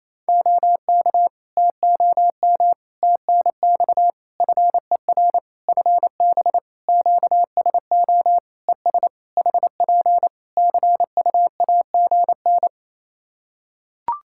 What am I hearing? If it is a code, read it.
Morse code, heard as a keyed tone, about 28 wpm, 700 Hz. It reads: OK TOM TNX FER FB QSO ES HP CUAGN
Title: Practice QSO - Exchange 4 - Ending - 28wpm